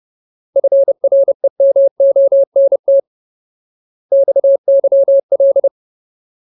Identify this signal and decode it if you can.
Morse code, heard as a keyed tone, about 30 wpm, 550 Hz. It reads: FREMONT XYL
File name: Ultimate Word List / 30WPM Ultimate